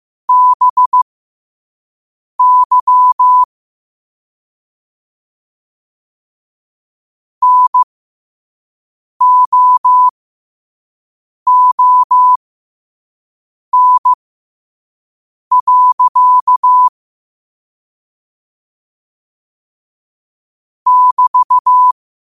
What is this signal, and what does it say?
Morse code, heard as a keyed tone, about 15 wpm, spaced as for 5 wpm, 1000 Hz. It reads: BY NOON. <BT>
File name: qotdfiles / 5wpm